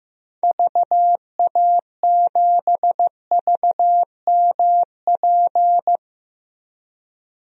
Morse code, heard as a keyed tone, about 16 wpm, 700 Hz. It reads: VA7VMP
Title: DXCC-100 Call Signs - 6 Characters - Repeated 2x - 15wpm